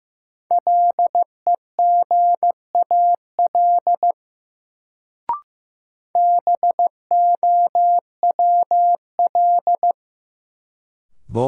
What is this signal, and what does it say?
Morse code, heard as a keyed tone, about 16 wpm, 700 Hz. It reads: LEGAL BOWL
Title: Sentences from Top 1200 Words - New Word Drill - 15wpm